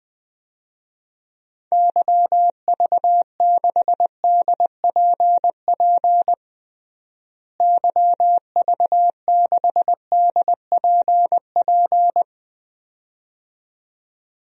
Morse code, heard as a keyed tone, about 21 wpm, 700 Hz. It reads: YV6DPP YV6DPP
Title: DXCC-100 Call Signs - 6 Characters - Repeated 2x - 20wpm